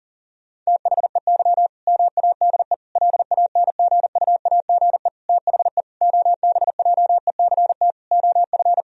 Morse code, heard as a keyed tone, about 40 wpm, 700 Hz. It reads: THEY MADE LANGUAGE THE OBJECT OF
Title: Sentences from Top 1000 Words - Review Part 3 - 40wpm